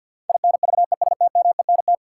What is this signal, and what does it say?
Morse code, heard as a keyed tone, about 52 wpm, 700 Hz. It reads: INVESTMENT
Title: Sentences from Top 700 Words - New Word Drill - 50wpm